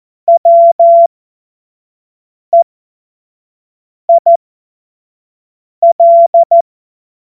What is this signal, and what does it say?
Morse code, heard as a keyed tone, about 14 wpm, spaced as for 5 wpm, 680 Hz. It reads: WEIL
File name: Morsekurs / Witze / Witz 25. Juli 2023 25 F70